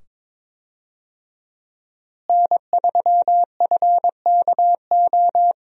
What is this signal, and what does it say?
Morse code, heard as a keyed tone, about 23 wpm, 700 Hz. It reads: N3FKO
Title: USA Call Signs - 5 Characters - Repeated 2x - 22wpm